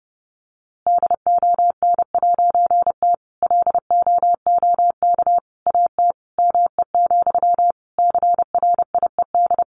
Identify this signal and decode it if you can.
Morse code, heard as a keyed tone, about 30 wpm, 700 Hz. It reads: DON'T LOOK AT ME, CRIED
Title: A Sound Of Thunder 30wpm